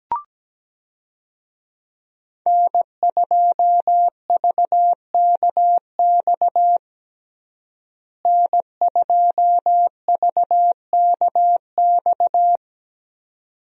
Morse code, heard as a keyed tone, about 17 wpm, 700 Hz. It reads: N2VKX N2VKX
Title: USA Call Signs - 5 Characters - Repeated 2x - 17wpm